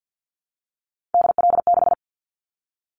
Morse code, heard as a keyed tone, about 50 wpm, 700 Hz. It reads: DL6
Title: Callsign Prefixes for Top 25 DXCC - Repeated 3x - 50wpm